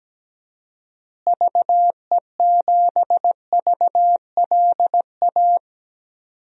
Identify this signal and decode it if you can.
Morse code, heard as a keyed tone, about 17 wpm, 700 Hz. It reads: VE7VLA
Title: DXCC-100 Call Signs - 6 Characters - Repeated 2x - 17wpm